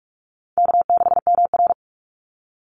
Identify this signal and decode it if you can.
Morse code, heard as a keyed tone, about 44 wpm, 700 Hz. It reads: K6MR
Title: Call Signs - 45wpm